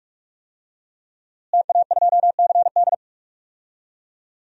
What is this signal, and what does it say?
Morse code, heard as a keyed tone, about 46 wpm, 700 Hz. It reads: TA2CD
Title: DXCC-100 Call Signs - 5 Characters - Repeated 2x - 45wpm